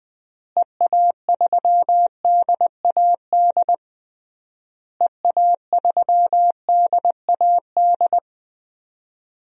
Morse code, heard as a keyed tone, about 21 wpm, 700 Hz. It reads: EA3DAD EA3DAD
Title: DXCC-100 Call Signs - 6 Characters - Repeated 3x - 20wpm